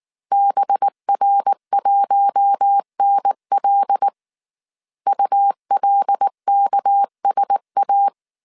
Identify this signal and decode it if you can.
Morse code, heard as a keyed tone, about 19 wpm, 800 Hz. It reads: BR1NL ULKSA
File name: CW / DET / 1 9 1500C S A 18wpm 2459